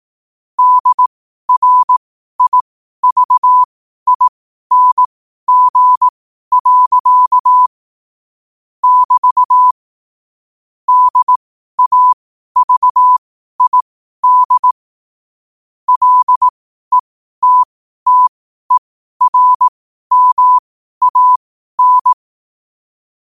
Morse code, heard as a keyed tone, about 18 wpm, spaced as for 12 wpm, 1000 Hz. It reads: DRIVING. <BT> DAVID LETTERMAN
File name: qotdfiles / 12wpm